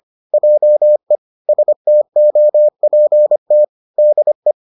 Morse code, heard as a keyed tone, about 25 wpm, 600 Hz. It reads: JE STOPT DE